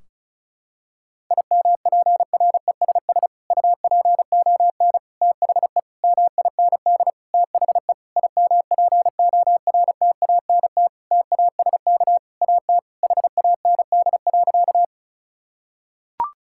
Morse code, heard as a keyed tone, about 34 wpm, 700 Hz. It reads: IMPRESS UPON THE MIND THE IMPORTANT TASK AT HAND.
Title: Sentences from Top 1600 Words 35wpm